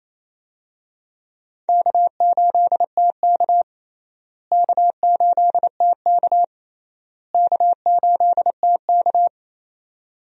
Morse code, heard as a keyed tone, about 28 wpm, 700 Hz. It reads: K8TK K8TK K8TK
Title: USA Call Signs - 4 Characters - Repeated 3x - 28wpm